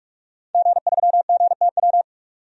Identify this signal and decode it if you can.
Morse code, heard as a keyed tone, about 46 wpm, 700 Hz. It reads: G3GTW
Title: DXCC-100 Call Signs - 5 Characters - 45wpm